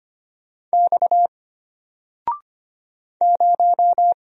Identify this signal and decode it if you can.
Morse code, heard as a keyed tone, about 25 wpm, 700 Hz. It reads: X 0
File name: Single-Letter-Number-25wpm